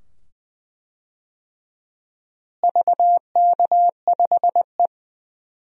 Morse code, heard as a keyed tone, about 20 wpm, 700 Hz. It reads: VK5E
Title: DXCC-100 Call Signs - 4 Characters - Repeated 2x - 20wpm